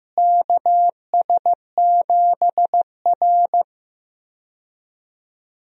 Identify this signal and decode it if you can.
Morse code, heard as a keyed tone, about 15 wpm, 700 Hz. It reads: KS7R
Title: USA Call Signs - 4 Characters - Repeated 3x - 15wpm